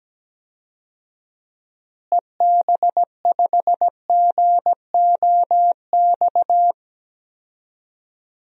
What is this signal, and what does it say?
Morse code, heard as a keyed tone, about 17 wpm, 700 Hz. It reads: EB5GOX
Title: DXCC-100 Call Signs - 6 Characters - Repeated 3x - 17wpm